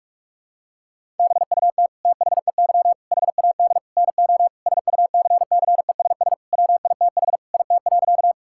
Morse code, heard as a keyed tone, about 42 wpm, 690 Hz. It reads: BUT THEY HAD NO SUCCESS WITH IT.
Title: Sentences from Top 1000 Words - Review Part 6 - 45wpm